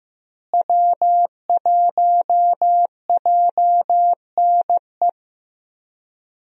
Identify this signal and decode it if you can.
Morse code, heard as a keyed tone, about 15 wpm, 700 Hz. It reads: W1JNE
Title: USA Call Signs - 5 Characters - Repeated 3x - 15wpm